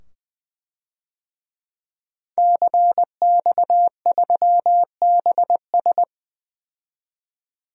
Morse code, heard as a keyed tone, about 20 wpm, 700 Hz. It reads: CX3BS
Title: DXCC-100 Call Signs - 5 Characters - 20wpm